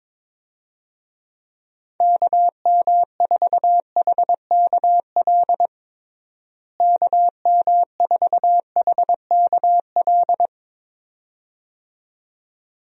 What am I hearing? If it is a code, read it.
Morse code, heard as a keyed tone, about 22 wpm, 700 Hz. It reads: KM4HKL KM4HKL
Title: USA Call Signs - 6 Characters - Repeated 2x - 22wpm